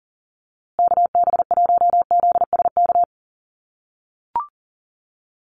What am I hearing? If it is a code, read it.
Morse code, heard as a keyed tone, about 40 wpm, 700 Hz. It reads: KB1ZSK
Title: USA Call Signs Repeated 2x 40wpm